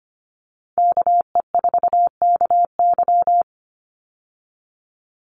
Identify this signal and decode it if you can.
Morse code, heard as a keyed tone, about 25 wpm, 700 Hz. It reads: KE4KY
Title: Call Signs Repeated 2x - 25wpm